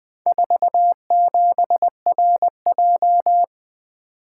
Morse code, heard as a keyed tone, about 20 wpm, 700 Hz. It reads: 47RJ
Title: Four Letter-Number Combinations - 20wpm